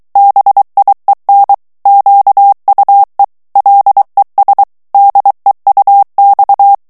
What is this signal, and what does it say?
Morse code, heard as a keyed tone, about 23 wpm, 800 Hz. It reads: BIEN QUE LES DEUX
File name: FRANCAIS TEXTE CLAIR 1 VITESSE 22 MOTS MINUTES